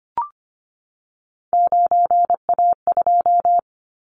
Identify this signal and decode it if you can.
Morse code, heard as a keyed tone, about 25 wpm, 700 Hz. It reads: 9A2
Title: Callsign Prefixes for Top 50 DXCC - 25wpm